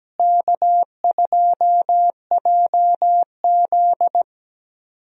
Morse code, heard as a keyed tone, about 17 wpm, 700 Hz. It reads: K2JZ
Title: USA Call Signs - 4 Characters - Repeated 3x - 17wpm